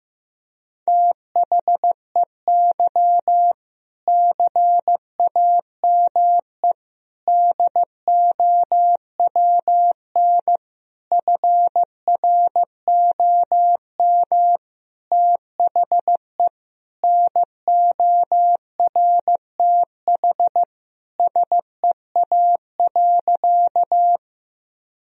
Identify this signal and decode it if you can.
Morse code, heard as a keyed tone, about 15 wpm, 700 Hz. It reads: THEY CAME DOWN FROM THE NORTH SEA.
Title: Sentences from Top 1000 Words - Review Part 1 - 15wpm